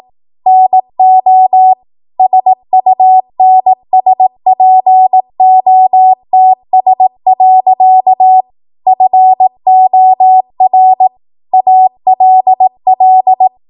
Morse code, heard as a keyed tone, about 18 wpm, 750 Hz. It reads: NO SUNSPOTS. FOR ALL
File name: Archive / 18 WPM / 18WPM